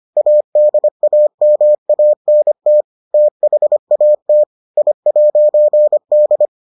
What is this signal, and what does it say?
Morse code, heard as a keyed tone, about 25 wpm, 600 Hz. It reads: ADAMANT THAT I'D